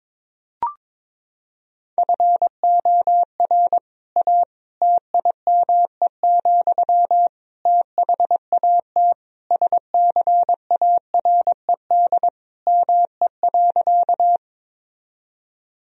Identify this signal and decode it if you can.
Morse code, heard as a keyed tone, about 22 wpm, 700 Hz. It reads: FOR A TIME, THAT SCARED ME.
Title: Sentences from Top 1000 Words - Review Part 4 - 22wpm